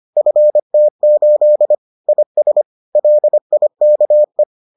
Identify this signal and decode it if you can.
Morse code, heard as a keyed tone, about 25 wpm, 600 Hz. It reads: FT8 IS LIKE